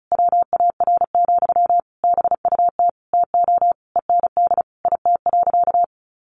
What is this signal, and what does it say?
Morse code, heard as a keyed tone, about 34 wpm, 700 Hz. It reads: WAR, BUT TO END IT.
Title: Sentences from Top 1200 Words 35wpm